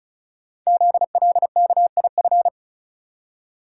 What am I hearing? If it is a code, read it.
Morse code, heard as a keyed tone, about 36 wpm, 690 Hz. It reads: ZLKIF
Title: Code Groups - 5 Letters - 35wpm